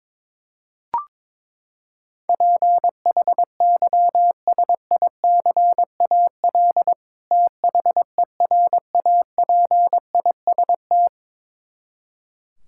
Morse code, heard as a keyed tone, about 22 wpm, 700 Hz. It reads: PHYSICAL THERAPIST
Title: Practice QSO - Exchange 3 - More Info - 22wpm